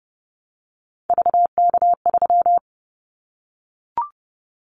Morse code, heard as a keyed tone, about 30 wpm, 700 Hz. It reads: VK3
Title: Callsign Prefixes for Top 50 DXCC - 30wpm